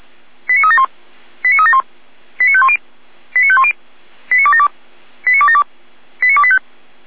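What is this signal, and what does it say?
Five-tone selective call (ZVEI), heard as quick runs of five tones, tones of about 70 ms. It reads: 89371 89371 89510 89510 89272 89272 89285
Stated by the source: ZVEI five-tone selective call received with an SDR.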